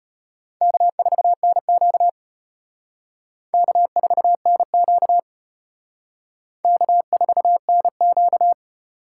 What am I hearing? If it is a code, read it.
Morse code, heard as a keyed tone, about 31 wpm, 700 Hz. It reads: K4NQ K4NQ K4NQ